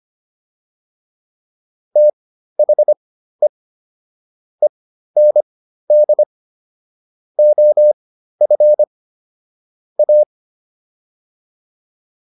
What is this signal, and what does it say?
Morse code, heard as a keyed tone, about 25 wpm, spaced as for 13 wpm, 600 Hz. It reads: THE END OF A